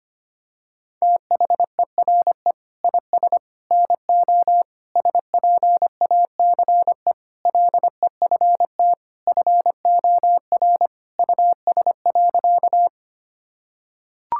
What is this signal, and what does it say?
Morse code, heard as a keyed tone, about 25 wpm, 700 Hz. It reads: THERE IS NO SPACE LEFT FOR US.
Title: Sentences from Top 1000 Words - Review Part 5 - 25wpm